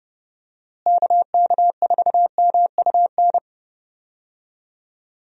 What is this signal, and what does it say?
Morse code, heard as a keyed tone, about 30 wpm, 700 Hz. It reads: KK4MUN
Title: USA Call Signs - 6 Characters - Repeated 2x - 30wpm